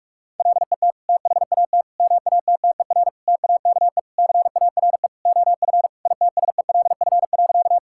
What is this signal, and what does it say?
Morse code, heard as a keyed tone, about 46 wpm, 700 Hz. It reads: LET THAT MATTER TAKE CARE OF ITSELF.
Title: Sentences from Top 1000 Words - Review Part 5 - 45wpm